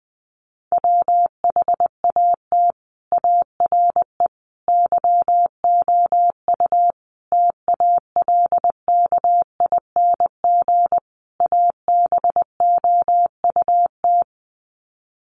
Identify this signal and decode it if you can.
Morse code, heard as a keyed tone, about 20 wpm, 700 Hz. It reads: WHAT ARE YOU TALKING ABOUT
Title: Sets of 5 Words 20wpm